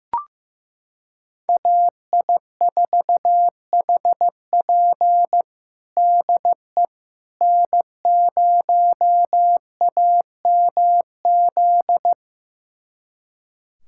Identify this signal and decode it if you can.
Morse code, heard as a keyed tone, about 15 wpm, 700 Hz. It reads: AI4HP DE N0AMZ
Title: USA Callsign DE Callsign - 5 Characters - 15wpm